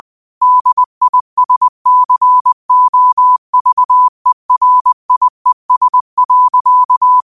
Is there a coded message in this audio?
Yes: Morse code from 0.4 to 7.2 s, heard as a keyed tone, about 20 wpm, 1000 Hz. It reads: DISCOVERIES.